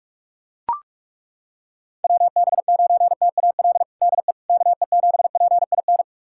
Morse code, heard as a keyed tone, about 42 wpm, 700 Hz. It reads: WB9TAL DE KE7PIN
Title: USA Callsign DE Callsign - 6 Characters - 45wpm